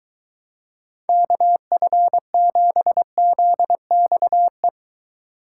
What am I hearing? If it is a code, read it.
Morse code, heard as a keyed tone, about 22 wpm, 700 Hz. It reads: KF7ZXE